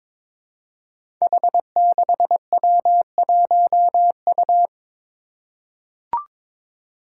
Morse code, heard as a keyed tone, about 22 wpm, 700 Hz. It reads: H6W1U
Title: Five Challenging Letter-Number Combinations 22wpm